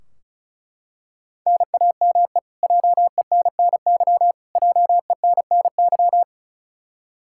Morse code, heard as a keyed tone, about 36 wpm, 700 Hz. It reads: NAME JENNY JENNY
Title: Practice QSO - Exchange 1 - Beginning - 35wpm